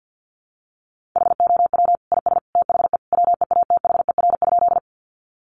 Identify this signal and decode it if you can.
Morse code, heard as a keyed tone, about 50 wpm, 700 Hz. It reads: HOW IS THE WEATHER?
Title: Sentences from Top 800 Words 50wpm